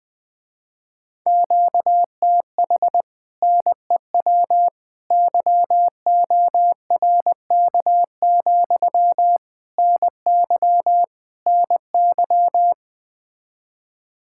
Morse code, heard as a keyed tone, about 20 wpm, 700 Hz. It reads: QTH NEW YORK, NY NY
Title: Speed-Racing Common QSO Phrases Encore - 17wpm